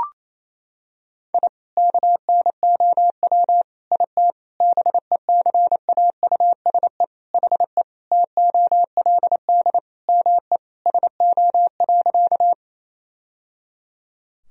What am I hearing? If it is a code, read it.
Morse code, heard as a keyed tone, about 27 wpm, 700 Hz. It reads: I KNOW IT BECAUSE HE TOLD ME SO.
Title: Sentences from Top 1000 Words - Review Part 3 - 28wpm